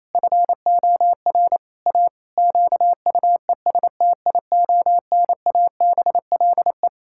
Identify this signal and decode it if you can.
Morse code, heard as a keyed tone, about 28 wpm, 700 Hz. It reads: FOR A QUESTIONABLE
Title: Pangram Challenge v2 - 28wpm